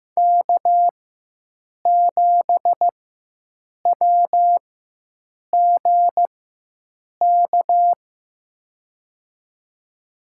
Morse code, heard as a keyed tone, about 15 wpm, spaced as for 7 wpm, 700 Hz. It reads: K7WGK
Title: USA Call Signs - 5 Characters - 4x Character Spacing - 15wpm